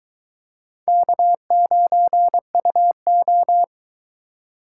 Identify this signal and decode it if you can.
Morse code, heard as a keyed tone, about 23 wpm, 700 Hz. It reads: K9UO